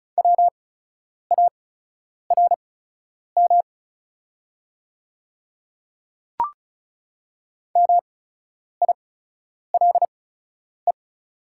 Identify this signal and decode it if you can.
Morse code, heard as a keyed tone, about 36 wpm, spaced as for 10 wpm, 690 Hz. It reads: WARM MILE
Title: 8x Character Spacing - Top 500 Words - 35wpm